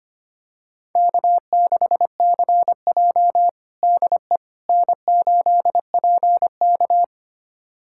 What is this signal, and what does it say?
Morse code, heard as a keyed tone, about 25 wpm, 700 Hz. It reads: K6CJ DE N8PK
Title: USA Callsign DE Callsign - 4 Characters - 25wpm